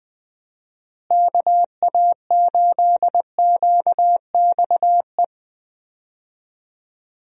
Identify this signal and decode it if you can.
Morse code, heard as a keyed tone, about 20 wpm, 700 Hz. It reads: KA8QXE
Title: USA Call Signs - 6 Characters - Repeated 3x - 20wpm